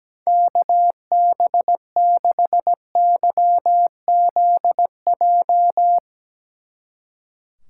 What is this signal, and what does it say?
Morse code, heard as a keyed tone, about 17 wpm, 700 Hz. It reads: KB6YZJ
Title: Speed Racing - USA 2x3 Call Signs 17wpm